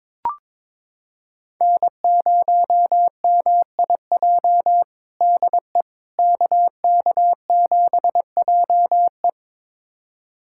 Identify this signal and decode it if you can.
Morse code, heard as a keyed tone, about 22 wpm, 700 Hz. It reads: N0MIJ DE KK7JE
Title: USA Callsign DE Callsign - 5 Characters - 22wpm